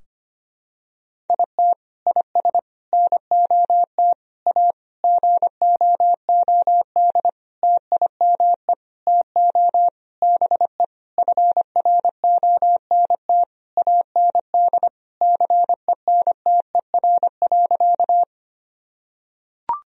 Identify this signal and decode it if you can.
Morse code, heard as a keyed tone, about 25 wpm, 700 Hz. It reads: IT IS NOT A GOOD TIME TO BE FRONT AND CENTER.
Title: Sentences from Top 1000 Words - Review Part 2 - 25wpm